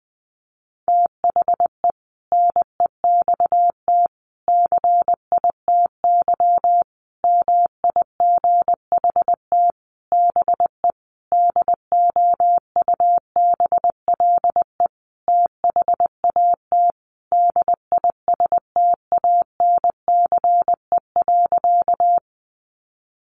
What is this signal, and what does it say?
Morse code, heard as a keyed tone, about 19 wpm, 700 Hz. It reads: THE NEXT CITY MIGHT BE DOUBLE THAT DISTANCE.
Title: Sentences from Top 1400 Words 20wpm